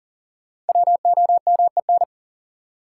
Morse code, heard as a keyed tone, about 40 wpm, 700 Hz. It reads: WOMEN
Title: Sentences from Top 1300 Words - New Word Drill - 40wpm